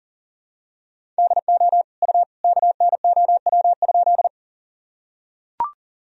Morse code, heard as a keyed tone, about 40 wpm, 700 Hz. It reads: DO U KNOW?
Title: Sentences from Top 100 Words and Abbreviations 40wpm